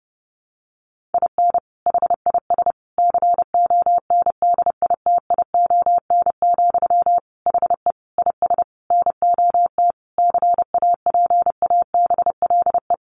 Morse code, heard as a keyed tone, about 30 wpm, 700 Hz. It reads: IN HIS CONDITION, HE IS NOT CAPABLE
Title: Sentences from Top 1200 Words 30wpm